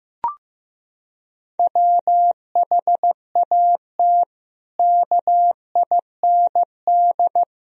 Morse code, heard as a keyed tone, about 15 wpm, 700 Hz. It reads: WHAT KIND
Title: Sentences from Top 1000 Words - Review Part 4 - 15wpm